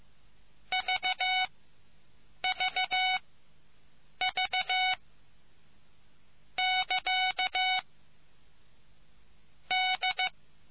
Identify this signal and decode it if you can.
Morse code, heard as a keyed tone, about 15 wpm, spaced as for 7 wpm, 2250 Hz. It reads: VVV <KA> D